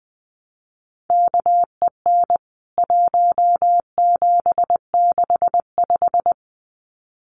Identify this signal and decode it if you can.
Morse code, heard as a keyed tone, about 20 wpm, 700 Hz. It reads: KEN 1765
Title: CWT Exchange 20wpm